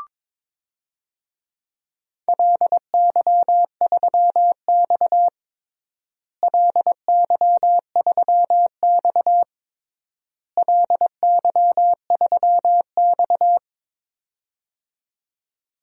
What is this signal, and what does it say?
Morse code, heard as a keyed tone, about 21 wpm, 700 Hz. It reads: LY3X LY3X LY3X
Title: DXCC-100 Call Signs - 4 Characters - Repeated 3x - 22wpm